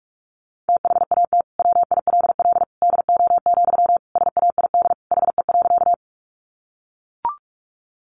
Morse code, heard as a keyed tone, about 44 wpm, 700 Hz. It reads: THAT WILL DO, SAID HE.
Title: Sentences from Top 100 Words 45wpm